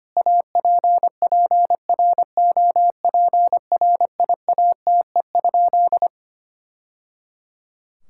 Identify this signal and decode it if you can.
Morse code, heard as a keyed tone, about 25 wpm, 700 Hz. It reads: APPROPRIATE?
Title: Sentences from Top 1000 Words - Review Part 5 - 25wpm